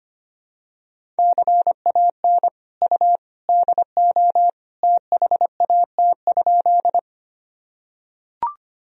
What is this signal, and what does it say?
Morse code, heard as a keyed tone, about 25 wpm, 700 Hz. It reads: CAN U DO THAT?
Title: Sentences from Top 100 Words and Abbreviations 25wpm